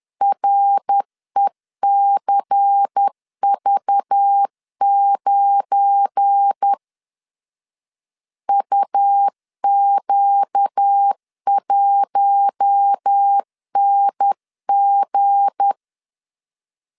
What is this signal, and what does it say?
Morse code, heard as a keyed tone, about 10 wpm, 800 Hz. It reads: RECV9 UQ1NG
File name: CW / DET / 1 9 1500C S B 10wpm 4439